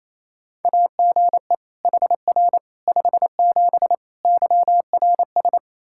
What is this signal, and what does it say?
Morse code, heard as a keyed tone, about 28 wpm, 700 Hz. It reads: AGE HR 57 YRS
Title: K1USN SST QSO Practice 28wpm